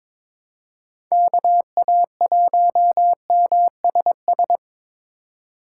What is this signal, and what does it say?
Morse code, heard as a keyed tone, about 22 wpm, 700 Hz. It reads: KA1MSS
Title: USA Call Signs - 6 Characters - 3x Character Spacing - 22wpm